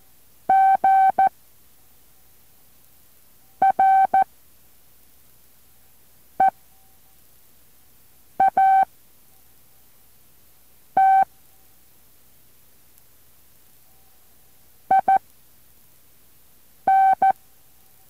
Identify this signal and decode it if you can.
Morse code, heard as a keyed tone, about 14 wpm, spaced as for 4 wpm, 770 Hz. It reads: GREAT IN